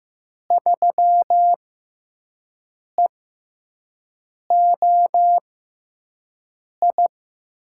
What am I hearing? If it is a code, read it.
Morse code, heard as a keyed tone, about 15 wpm, spaced as for 5 wpm, 700 Hz. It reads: 3EOI
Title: USA Call Signs - 6 Characters - 6x Character Spacing - 15wpm